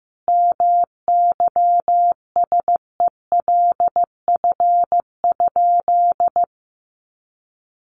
Morse code, heard as a keyed tone, about 15 wpm, 700 Hz. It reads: MYSELF?
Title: Sentences from Top 1300 Words 15wpm